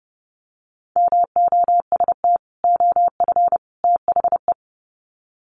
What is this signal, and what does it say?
Morse code, heard as a keyed tone, about 30 wpm, 700 Hz. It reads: MOST OF THE
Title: Sets of 3 Words 30wpm